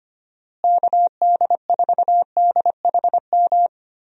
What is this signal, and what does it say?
Morse code, heard as a keyed tone, about 25 wpm, 700 Hz. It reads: KD4DHM
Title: USA Call Signs - 6 Characters - Repeated 2x - 25wpm